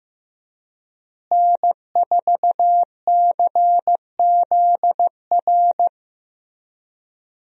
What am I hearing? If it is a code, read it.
Morse code, heard as a keyed tone, about 15 wpm, 700 Hz. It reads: N4CZR
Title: USA Call Signs - 5 Characters - Repeated 2x - 15wpm